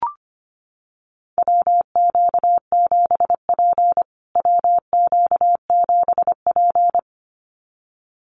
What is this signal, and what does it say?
Morse code, heard as a keyed tone, about 25 wpm, 700 Hz. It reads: WQ7P WQ7P
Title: USA Call Signs Repeated 2x 25wpm